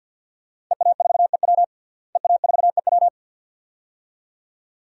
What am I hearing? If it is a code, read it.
Morse code, heard as a keyed tone, about 50 wpm, 700 Hz. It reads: EA4EW EA4EW
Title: DXCC-100 Call Signs - 5 Characters - Repeated 2x - 50wpm